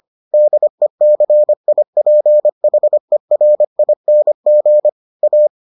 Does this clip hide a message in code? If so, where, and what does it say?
Yes: Morse code from 0.3 to 5.5 s, heard as a keyed tone, about 25 wpm, 600 Hz. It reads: DECIPHERING A